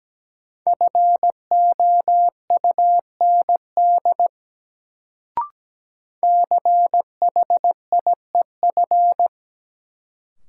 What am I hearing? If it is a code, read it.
Morse code, heard as a keyed tone, about 17 wpm, 700 Hz. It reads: FOUND CHIEF
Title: Sets of 5 Words Encore - New Word Drill - 17wpm